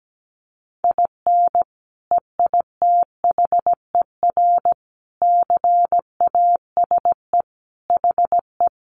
Morse code, heard as a keyed tone, about 17 wpm, 700 Hz. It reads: IN EITHER CASE HE
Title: Sentences from Top 1300 Words 17wpm